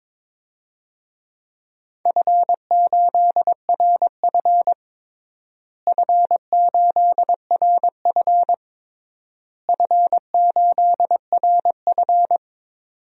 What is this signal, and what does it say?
Morse code, heard as a keyed tone, about 22 wpm, 700 Hz. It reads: F8RF F8RF F8RF
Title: DXCC-100 Call Signs - 4 Characters - Repeated 3x - 22wpm